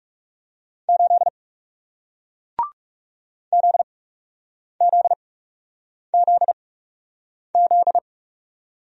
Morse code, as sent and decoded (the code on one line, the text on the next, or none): ---.. / --.. --.. --.. --..
8 ZZZZ